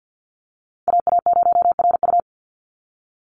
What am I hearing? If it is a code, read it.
Morse code, heard as a keyed tone, about 50 wpm, 700 Hz. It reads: AA0RU
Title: USA Call Signs 50wpm